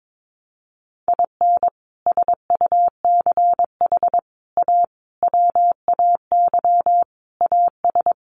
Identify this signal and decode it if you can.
Morse code, heard as a keyed tone, about 22 wpm, 700 Hz. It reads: IN SUCH A WAY AS
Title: Sets of 5 Words Encore 22wpm